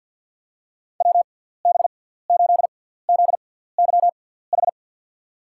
Morse code, heard as a keyed tone, about 52 wpm, spaced as for 17 wpm, 700 Hz. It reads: WB8ZYH
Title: USA Call Signs - 6 Characters - 6x Character Spacing - 50wpm